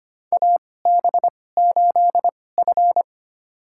Morse code, heard as a keyed tone, about 25 wpm, spaced as for 18 wpm, 700 Hz. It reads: AB8F
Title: USA Call Signs - 2x Character Spacing - 25wpm